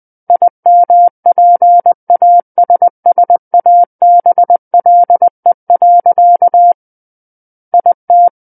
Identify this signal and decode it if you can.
Morse code, heard as a keyed tone, about 20 wpm, 700 Hz. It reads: IMPASSABLE. IT
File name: 20 WPM